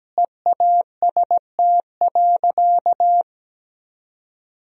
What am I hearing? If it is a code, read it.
Morse code, heard as a keyed tone, about 17 wpm, 700 Hz. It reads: EAST.
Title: Sentences from Top 1000 Words - Review Part 3 - 17wpm